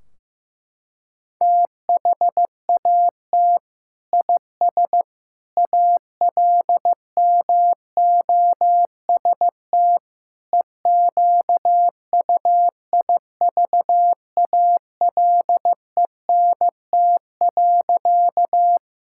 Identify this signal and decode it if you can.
Morse code, heard as a keyed tone, about 15 wpm, 700 Hz. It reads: THAT IS ALMOST EQUIVALENT.
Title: Sentences from Top 1600 Words 15wpm